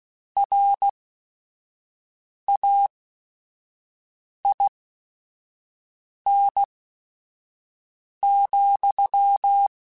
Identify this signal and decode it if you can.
Morse code, heard as a keyed tone, about 16 wpm, spaced as for 5 wpm, 800 Hz. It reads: RAIN,
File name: Weather Sets / Weather Sets-mp3s / Weather 5 5 @5